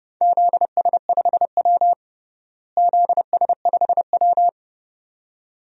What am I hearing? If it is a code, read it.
Morse code, heard as a keyed tone, about 32 wpm, 700 Hz. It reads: ZS5W ZS5W
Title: DXCC-100 Call Signs - 4 Characters - Repeated 2x - 30wpm